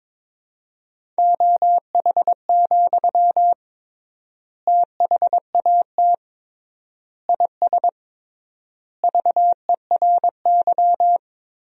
Morse code, heard as a keyed tone, about 22 wpm, 700 Hz. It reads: OH, THAT IS VERY
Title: 3x Word-Spacing - Sentences from Top 500 Words - 22wpm